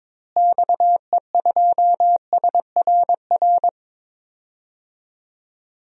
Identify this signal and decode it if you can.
Morse code, heard as a keyed tone, about 22 wpm, 700 Hz. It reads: XE2SRR
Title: DXCC-100 Call Signs - 6 Characters - Repeated 3x - 22wpm